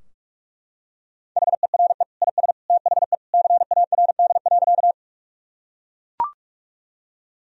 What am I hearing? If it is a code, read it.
Morse code, heard as a keyed tone, about 41 wpm, 700 Hz. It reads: HERE IS THE CARD.
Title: Sentences from Top 1000 Words - Review Part 4 - 45wpm